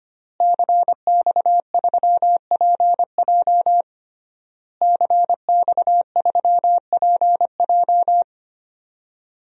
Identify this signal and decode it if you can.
Morse code, heard as a keyed tone, about 25 wpm, 700 Hz. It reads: CX3PJ CX3PJ
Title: DXCC-100 Call Signs - 5 Characters - Repeated 3x - 25wpm